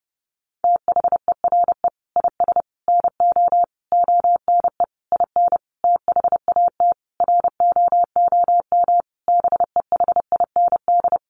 Morse code, heard as a keyed tone, about 30 wpm, 700 Hz. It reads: THERE IS NO ONE IN THAT ROOM BEHIND
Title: Sentences from Top 500 Words 30wpm